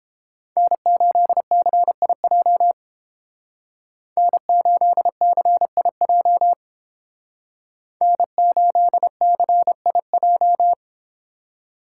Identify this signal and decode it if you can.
Morse code, heard as a keyed tone, about 30 wpm, 700 Hz. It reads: N8CIJ N8CIJ N8CIJ